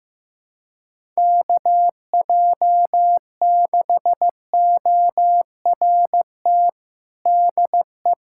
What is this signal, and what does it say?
Morse code, heard as a keyed tone, about 16 wpm, 700 Hz. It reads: KJ6ORT DE
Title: USA Callsign DE Callsign - 6 Characters - 15wpm